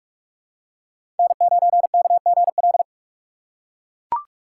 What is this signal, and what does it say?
Morse code, heard as a keyed tone, about 42 wpm, 700 Hz. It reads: N9KGL
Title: USA Call Signs - 5 Characters - 6x Character Spacing - 45wpm